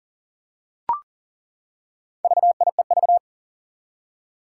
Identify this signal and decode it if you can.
Morse code, heard as a keyed tone, about 40 wpm, 700 Hz. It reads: VIEV
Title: Four Challenging Letter-Number Combinations 40wpm